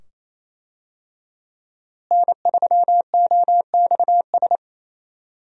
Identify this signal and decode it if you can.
Morse code, heard as a keyed tone, about 28 wpm, 700 Hz. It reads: N3OXS
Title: USA Call Signs - 5 Characters - Repeated 2x - 28wpm